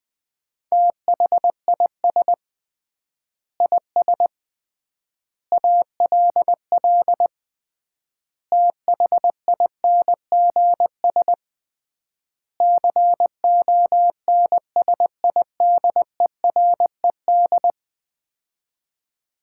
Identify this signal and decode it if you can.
Morse code, heard as a keyed tone, about 20 wpm, 700 Hz. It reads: THIS IS ALL THINGS CONSIDERED
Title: 3x Word-Spacing - Sets of 5 Words - 20wpm